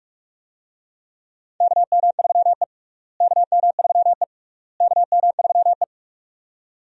Morse code, heard as a keyed tone, about 43 wpm, 700 Hz. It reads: KM3E KM3E KM3E
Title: USA Call Signs - 4 Characters - Repeated 3x - 45wpm